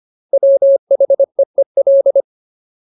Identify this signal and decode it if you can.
Morse code, heard as a keyed tone, about 25 wpm, 550 Hz. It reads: WHEEL
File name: Ultimate Word List / 25WPM Ultimate